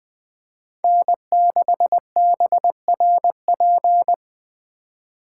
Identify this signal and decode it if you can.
Morse code, heard as a keyed tone, about 20 wpm, 700 Hz. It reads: N6BRP